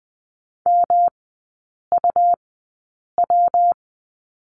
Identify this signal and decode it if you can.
Morse code, heard as a keyed tone, about 20 wpm, spaced as for 8 wpm, 700 Hz. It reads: MUW
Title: 191 Lesson 31e - Letter v - Three Characters 20wpm